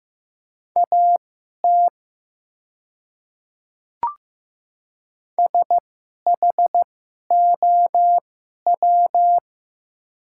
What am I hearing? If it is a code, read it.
Morse code, heard as a keyed tone, about 15 wpm, spaced as for 11 wpm, 700 Hz. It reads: AT SHOW
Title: 2x Character Spacing - Top 500 Words - 15wpm